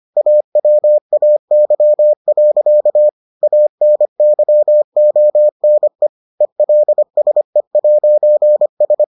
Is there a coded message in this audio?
Yes: Morse code from 0.2 to 9.0 s, heard as a keyed tone, about 25 wpm, 600 Hz. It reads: AWAY. ANYONE ELSE'S